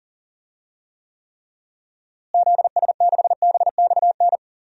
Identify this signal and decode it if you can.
Morse code, heard as a keyed tone, about 39 wpm, 700 Hz. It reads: ZS6BXN
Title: DXCC-100 Call Signs - 6 Characters - Repeated 3x - 40wpm